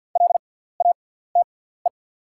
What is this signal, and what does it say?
Morse code, heard as a keyed tone, about 52 wpm, spaced as for 17 wpm, 690 Hz. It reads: LATE
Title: 6x Character Spacing - Top 500 Words - 50wpm